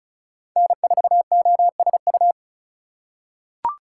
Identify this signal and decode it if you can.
Morse code, heard as a keyed tone, about 34 wpm, 700 Hz. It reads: N4OSU
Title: USA Call Signs - 5 Characters - 3x Character Spacing - 35wpm